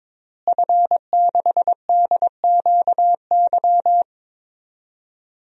Morse code, heard as a keyed tone, about 22 wpm, 700 Hz. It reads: F6DQY
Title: DXCC-100 Call Signs - 5 Characters - Repeated 3x - 22wpm